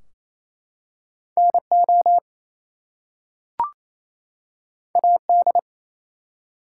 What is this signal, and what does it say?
Morse code, heard as a keyed tone, about 29 wpm, 700 Hz. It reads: NO AD
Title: Two Letter Words - 28wpm